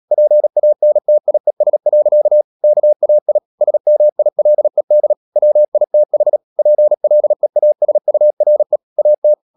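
Morse code, heard as a keyed tone, about 37 wpm, 600 Hz. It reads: PANTIES. KAI SMILED WITH PLEASURE AT
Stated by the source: Morse practice at 36 wpm, text of Ken Follett's novel Never, from CW med Gnister.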